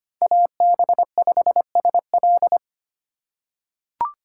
Morse code, heard as a keyed tone, about 25 wpm, 700 Hz. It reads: AB5SL
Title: USA Call Signs - 5 Characters - 2x Character Spacing - 25wpm